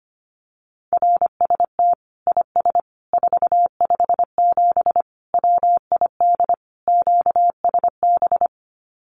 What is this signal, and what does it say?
Morse code, heard as a keyed tone, about 25 wpm, 700 Hz. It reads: RST IS 457 WID QSB
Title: Common QSO Phrases 25wpm